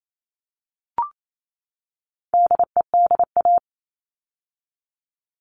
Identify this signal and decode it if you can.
Morse code, heard as a keyed tone, about 28 wpm, 700 Hz. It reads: DEDA
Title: All Cut Numbers - 4-Digit 28wpm